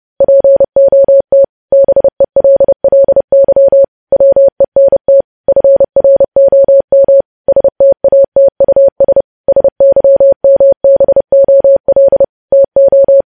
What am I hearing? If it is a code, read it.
Morse code, heard as a keyed tone, about 30 wpm, 550 Hz. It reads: POT BELLY WENT FROM STATUS SYMBOL TO
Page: news headlines in morse code at 30 wpm